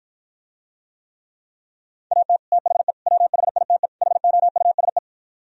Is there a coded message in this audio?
Yes: Morse code from 2.1 to 5.0 s, heard as a keyed tone, about 52 wpm, 700 Hz. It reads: AT THE WHITE HOUSE